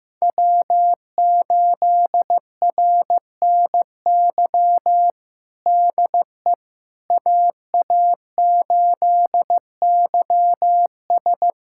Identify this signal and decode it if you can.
Morse code, heard as a keyed tone, about 15 wpm, 700 Hz. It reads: W8RNY DE AA8YS
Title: USA Callsign DE Callsign 15wpm